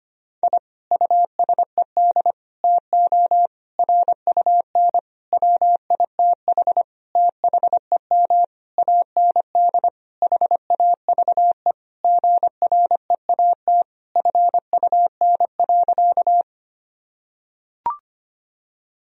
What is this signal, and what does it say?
Morse code, heard as a keyed tone, about 25 wpm, 700 Hz. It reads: I USED TO RUN WITH THEM AND HAVE GREAT FUN.
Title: Sentences from Top 1000 Words - Review Part 4 - 25wpm